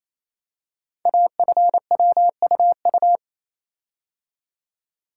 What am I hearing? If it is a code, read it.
Morse code, heard as a keyed tone, about 29 wpm, 690 Hz. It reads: AFWUU
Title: Code Groups - 5 Letters - 28wpm